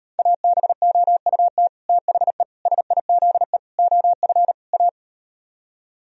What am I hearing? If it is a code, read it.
Morse code, heard as a keyed tone, about 38 wpm, 700 Hz. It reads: ABOUT THE SIZE OF A